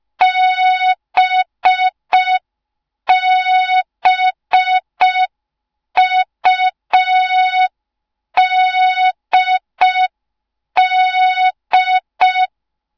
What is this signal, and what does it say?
Morse code, heard as a keyed tone, about 5 wpm, 3700 Hz. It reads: BBUDD
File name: Morze-HA2MN / hang / 06BDU HA2MN